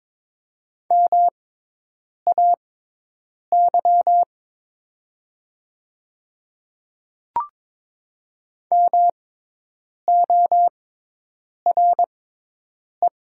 Morse code, heard as a keyed tone, about 22 wpm, spaced as for 8 wpm, 700 Hz. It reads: MAY MORE
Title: 6x Character Spacing - Top 200 Words - 22wpm